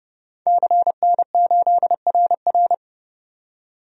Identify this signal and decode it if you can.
Morse code, heard as a keyed tone, about 31 wpm, 700 Hz. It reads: CN8RR
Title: DXCC-100 Call Signs - 5 Characters - 30wpm